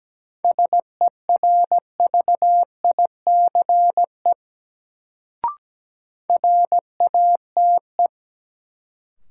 Morse code, heard as a keyed tone, about 17 wpm, 700 Hz. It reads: SERVICE RATE
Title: Sentences from Top 700 Words - New Word Drill - 17wpm